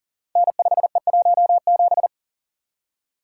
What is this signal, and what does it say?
Morse code, heard as a keyed tone, about 41 wpm, 700 Hz. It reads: N5E17
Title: Code Groups - 5 Letter:Numbers - 40wpm